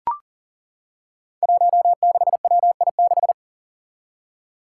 Morse code, heard as a keyed tone, about 40 wpm, 700 Hz. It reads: 16WI6
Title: Five Challenging Letter-Number Combinations 40wpm